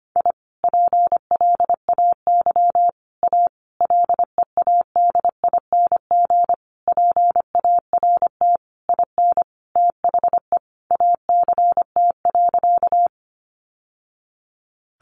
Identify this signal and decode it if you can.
Morse code, heard as a keyed tone, about 25 wpm, 700 Hz. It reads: I PLAY A LEADING PART IN THE ACT.
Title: Sentences from Top 1100 Words 25wpm